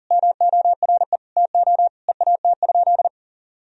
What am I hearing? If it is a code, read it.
Morse code, heard as a keyed tone, about 40 wpm, 700 Hz. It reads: MORE TO EAT?
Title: Sentences from Top 300 Words and Abbreviations 40wpm